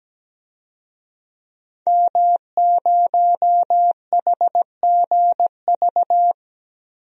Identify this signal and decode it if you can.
Morse code, heard as a keyed tone, about 17 wpm, 700 Hz. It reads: M0HGV
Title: DXCC-100 Call Signs - 5 Characters - Repeated 3x - 17wpm